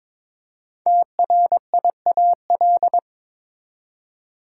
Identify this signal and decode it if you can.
Morse code, heard as a keyed tone, about 22 wpm, 700 Hz. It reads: TRIAL
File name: 5 Letter Words - First 3 Letters Given - 22wpm